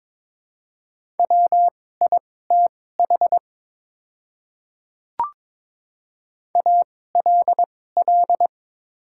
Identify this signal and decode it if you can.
Morse code, heard as a keyed tone, about 23 wpm, spaced as for 16 wpm, 700 Hz. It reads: WITH ALL
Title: 2x Character Spacing - Top 100 Words - 22wpm